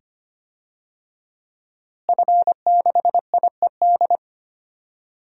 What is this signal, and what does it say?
Morse code, heard as a keyed tone, about 25 wpm, 700 Hz. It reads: F6IED
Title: DXCC-100 Call Signs - 5 Characters - Repeated 2x - 25wpm